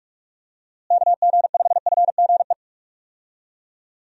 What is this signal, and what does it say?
Morse code, heard as a keyed tone, about 41 wpm, 700 Hz. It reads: KG5FGE
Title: USA Call Signs - 6 Characters - Repeated 2x - 45wpm